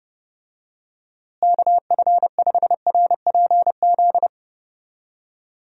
Morse code, heard as a keyed tone, about 30 wpm, 700 Hz. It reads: KF5RPZ
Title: USA Call Signs - 6 Characters - Repeated 2x - 30wpm